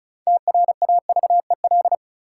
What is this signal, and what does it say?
Morse code, heard as a keyed tone, about 36 wpm, 700 Hz. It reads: TRAVEL
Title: Sentences from Top 400 Words - New Word Drill - 35wpm